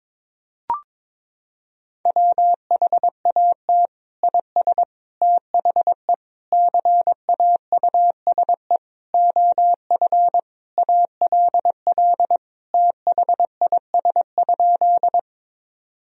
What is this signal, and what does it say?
Morse code, heard as a keyed tone, about 22 wpm, 700 Hz. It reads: WHAT IS THE CAUSE OF ALL THIS?
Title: Sentences from Top 1000 Words - Review Part 1 - 22wpm